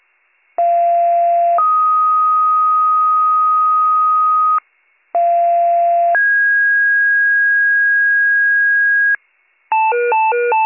Two-tone page, heard as a one-second tone, then a three-second tone, about 690 then 1250 Hz, about 690 then 1650 Hz, with noise. It is heard from 0.6 to 9.2 s.